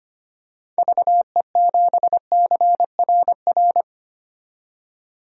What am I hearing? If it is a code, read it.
Morse code, heard as a keyed tone, about 25 wpm, 700 Hz. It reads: VE7CRR
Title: DXCC-100 Call Signs - 6 Characters - 25wpm